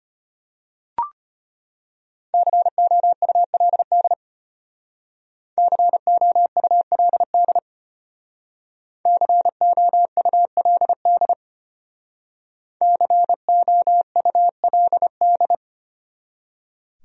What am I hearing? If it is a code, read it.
Morse code, heard as a keyed tone, about 32 wpm, 700 Hz. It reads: COULD COULD COULD COULD